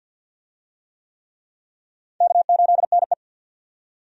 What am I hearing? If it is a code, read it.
Morse code, heard as a keyed tone, about 52 wpm, 700 Hz. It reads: K8NE
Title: USA Call Signs - 4 Characters - 50wpm